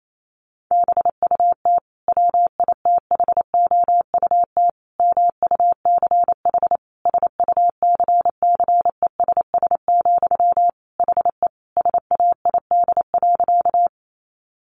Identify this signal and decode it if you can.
Morse code, heard as a keyed tone, about 28 wpm, 700 Hz. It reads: BUT WITHOUT MUCH SUCCESS, HE SAID.
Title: Sentences from Top 900 Words 28wpm